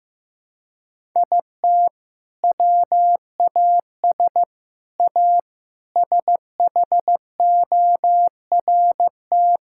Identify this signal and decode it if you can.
Morse code, heard as a keyed tone, about 15 wpm, 700 Hz. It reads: IT WAS A SHORT
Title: Sentences from Top 1600 Words 15wpm